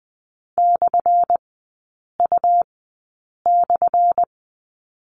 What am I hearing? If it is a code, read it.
Morse code, heard as a keyed tone, about 20 wpm, spaced as for 8 wpm, 700 Hz. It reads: /U/
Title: 178 Lesson 29e - Slash - Three Characters 20wpm